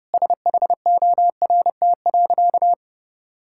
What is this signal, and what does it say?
Morse code, heard as a keyed tone, about 32 wpm, 700 Hz. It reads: SHORT.
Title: Sentences from Top 1000 Words - Review Part 6 - 30wpm